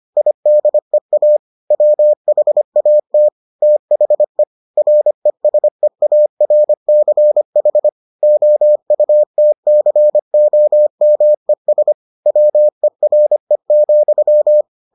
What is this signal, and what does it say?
Morse code, heard as a keyed tone, about 25 wpm, 600 Hz. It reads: IDEA WHAT THE RESEARCH OUTCOMES WERE,